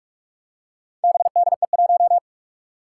Morse code, heard as a keyed tone, about 46 wpm, 700 Hz. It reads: BDE1
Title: Four Challenging Letter-Number Combinations 45wpm